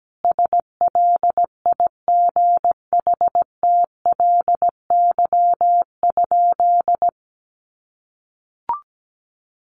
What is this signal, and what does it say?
Morse code, heard as a keyed tone, about 17 wpm, 700 Hz. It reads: SLIGHTLY?
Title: Sentences from Top 1300 Words 17wpm